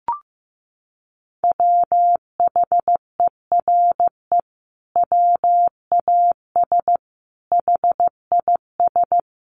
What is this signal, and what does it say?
Morse code, heard as a keyed tone, about 15 wpm, 700 Hz. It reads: WHERE WAS HIS
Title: Sentences from Top 1000 Words 15wpm